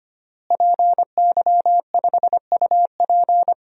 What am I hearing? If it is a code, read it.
Morse code, heard as a keyed tone, about 25 wpm, 700 Hz. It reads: PY5UP
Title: DXCC-100 Call Signs - 5 Characters - Repeated 3x - 25wpm